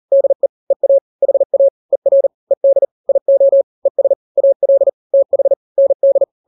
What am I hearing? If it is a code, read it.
Morse code, heard as a keyed tone, about 40 wpm, 550 Hz. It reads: DE EA HA ER ED IO ES AL TH ND
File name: n-grams / 2-grams 40 35 WPM